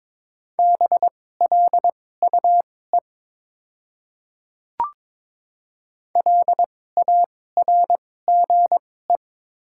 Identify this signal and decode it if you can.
Morse code, heard as a keyed tone, about 22 wpm, spaced as for 16 wpm, 700 Hz. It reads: BLUE LARGE
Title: 2x Character Spacing - Top 500 Words - 22wpm